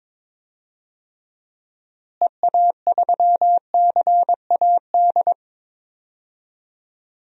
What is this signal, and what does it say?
Morse code, heard as a keyed tone, about 22 wpm, 700 Hz. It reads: EA3CAD
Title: DXCC-100 Call Signs - 6 Characters - Repeated 3x - 22wpm